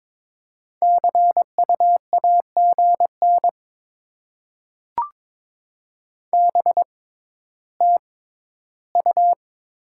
Morse code, heard as a keyed tone, about 22 wpm, spaced as for 12 wpm, 700 Hz. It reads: CU AGN BTU
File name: Top 20 QSO Elements - 6x Character Spacing - 22wpm